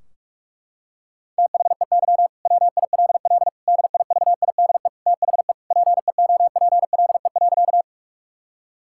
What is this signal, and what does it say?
Morse code, heard as a keyed tone, about 46 wpm, 700 Hz. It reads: THEY WILL DIVIDE THE PEOPLE.
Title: Sentences from Top 1600 Words 45wpm